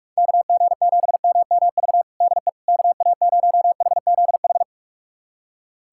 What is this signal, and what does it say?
Morse code, heard as a keyed tone, about 41 wpm, 700 Hz. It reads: KG7MMV DE KA0HZH
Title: USA Callsign DE Callsign 45wpm